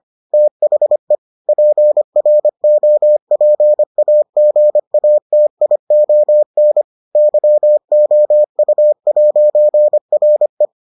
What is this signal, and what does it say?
Morse code, heard as a keyed tone, about 25 wpm, 600 Hz. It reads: THE PROPAGATION YOU'RE